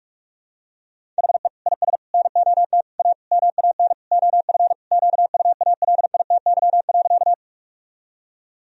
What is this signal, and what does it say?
Morse code, heard as a keyed tone, about 46 wpm, 700 Hz. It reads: HE IS NOT A MAN OF QUALITY.
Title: Sentences from Top 1000 Words - Review Part 4 - 45wpm